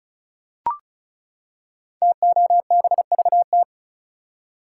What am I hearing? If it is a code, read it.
Morse code, heard as a keyed tone, about 36 wpm, 700 Hz. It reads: TOBVT
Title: Code Groups - 5 Letters - 35wpm